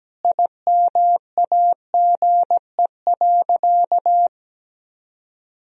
Morse code, heard as a keyed tone, about 17 wpm, 700 Hz. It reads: IMAGE.
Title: Sentences from Top 1000 Words - Review Part 4 - 17wpm